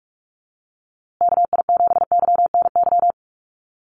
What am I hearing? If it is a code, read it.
Morse code, heard as a keyed tone, about 44 wpm, 700 Hz. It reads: KI7YNY
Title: USA Call Signs Repeated 3x 45wpm